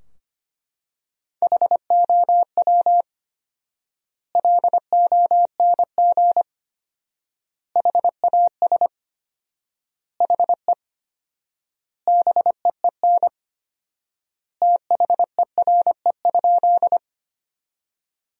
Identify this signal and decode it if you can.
Morse code, heard as a keyed tone, about 25 wpm, 700 Hz. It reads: HOW LONG HAS HE BEEN THERE?
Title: 4x Word-Spacing - Sentences from Top 100 Words - 25wpm